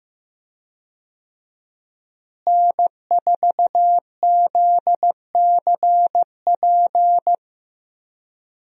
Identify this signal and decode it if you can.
Morse code, heard as a keyed tone, about 15 wpm, 700 Hz. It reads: N4ZCP
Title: USA Call Signs - 5 Characters - Repeated 2x - 15wpm